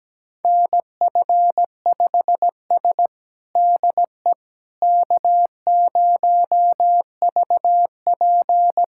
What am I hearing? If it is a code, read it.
Morse code, heard as a keyed tone, about 17 wpm, 700 Hz. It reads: NF5S DE K0VP
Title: USA Callsign DE Callsign - 4 Characters - 17wpm